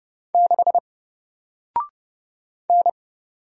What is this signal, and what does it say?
Morse code, heard as a keyed tone, about 30 wpm, 700 Hz. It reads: 6 N
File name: Single-Letter-Number-30wpm